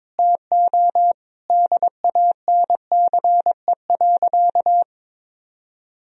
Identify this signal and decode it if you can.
Morse code, heard as a keyed tone, about 22 wpm, 700 Hz. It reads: TO DANCE.
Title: Sentences from Top 1000 Words - Review Part 4 - 22wpm